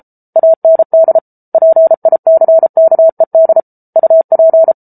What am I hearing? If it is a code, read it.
Morse code, heard as a keyed tone, about 34 wpm, 650 Hz. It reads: AND PICKED UP